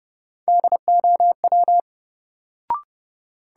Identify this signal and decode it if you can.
Morse code, heard as a keyed tone, about 30 wpm, 700 Hz. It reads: DOW
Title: Sets of 5 Words - New Word Drill - 30wpm